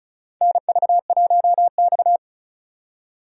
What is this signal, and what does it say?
Morse code, heard as a keyed tone, about 36 wpm, 700 Hz. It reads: NV1X
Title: USA Call Signs - 4 Characters - 35wpm